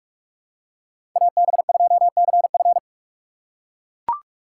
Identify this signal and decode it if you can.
Morse code, heard as a keyed tone, about 43 wpm, 700 Hz. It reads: AB2CF
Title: USA Call Signs - 5 Characters - 4x Character Spacing - 45wpm